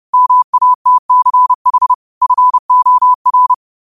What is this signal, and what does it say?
Morse code, heard as a keyed tone, about 30 wpm, 1000 Hz. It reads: MATCH FOR
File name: qotdfiles / 30wpm